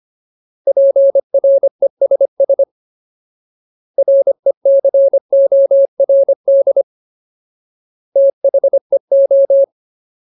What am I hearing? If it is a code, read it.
Morse code, heard as a keyed tone, about 25 wpm, 550 Hz. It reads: PRESS RECORD THEO
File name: Ultimate Word List / 25WPM Ultimate